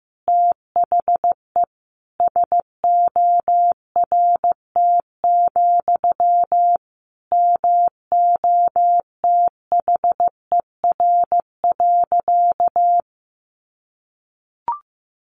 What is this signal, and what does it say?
Morse code, heard as a keyed tone, about 15 wpm, 700 Hz. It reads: THE SORT, MOTHER.
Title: Sentences from Top 1300 Words 15wpm